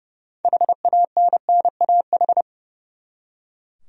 Morse code, heard as a keyed tone, about 30 wpm, 700 Hz. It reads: HANNAH
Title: USA Common Names - 30wpm